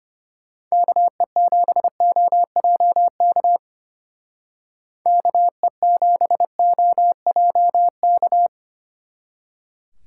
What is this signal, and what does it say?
Morse code, heard as a keyed tone, about 31 wpm, 700 Hz. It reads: KE7OJK KE7OJK